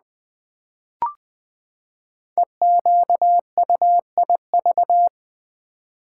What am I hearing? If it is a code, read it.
Morse code, heard as a keyed tone, about 20 wpm, 700 Hz. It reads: EQUIV
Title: Build A Word - 5 Letters At A Time - 10-Letter Words - 20wpm